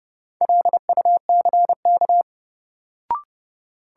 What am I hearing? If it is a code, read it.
Morse code, heard as a keyed tone, about 32 wpm, 700 Hz. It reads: LUCK
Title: Sentences from Top 1500 Words - New Word Drill - 30wpm